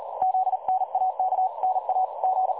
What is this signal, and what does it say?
Morse code, heard as a keyed tone, about 38 wpm, 750 Hz, with noise. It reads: ZNAK NA G
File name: dlazaawansowanych / nagraniazpasma / websdr recording 3565.0kHz